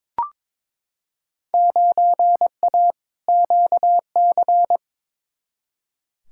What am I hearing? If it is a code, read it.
Morse code, heard as a keyed tone, about 21 wpm, 700 Hz. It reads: 9A QC
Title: Field Day Exchanges Encore - 22wpm